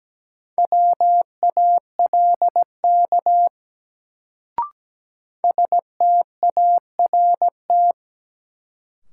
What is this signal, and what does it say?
Morse code, heard as a keyed tone, about 17 wpm, 700 Hz. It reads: WALK START
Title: Sentences from Top 300 Words - New Word Drill - 17wpm